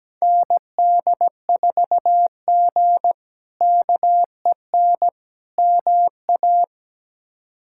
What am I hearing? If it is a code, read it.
Morse code, heard as a keyed tone, about 17 wpm, 700 Hz. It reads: ND4G KEN MA
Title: K1USN SST QSO Practice - 17wpm